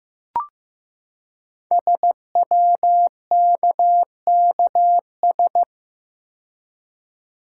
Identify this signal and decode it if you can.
Morse code, heard as a keyed tone, about 15 wpm, 700 Hz. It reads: SWKKS
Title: Code Groups - 5 Letters - 15wpm